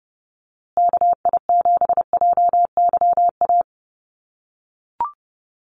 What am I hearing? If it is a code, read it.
Morse code, heard as a keyed tone, about 30 wpm, 700 Hz. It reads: KI7JYA
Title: USA Call Signs Repeated 3x 30wpm